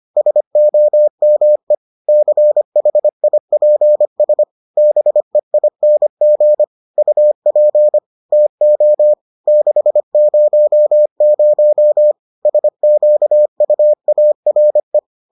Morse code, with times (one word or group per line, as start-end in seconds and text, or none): 0.2-1.7 SOME
2.1-4.4 CHIPS
4.8-6.6 BEING
7.0-8.0 UP
8.3-9.1 TO
9.5-12.1 600
12.5-15.0 SQUARE